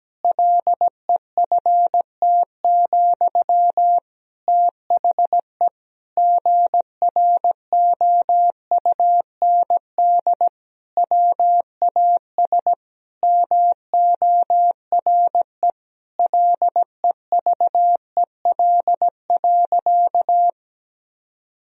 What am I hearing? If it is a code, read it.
Morse code, heard as a keyed tone, about 17 wpm, 700 Hz. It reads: LEFT, THE GROUND WAS MORE LEVEL.
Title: Sentences from Top 1000 Words - Review Part 4 - 17wpm